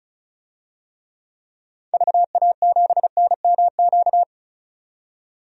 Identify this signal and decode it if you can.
Morse code, heard as a keyed tone, about 34 wpm, 700 Hz. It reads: VA7NMQ
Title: DXCC-100 Call Signs - 6 Characters - 35wpm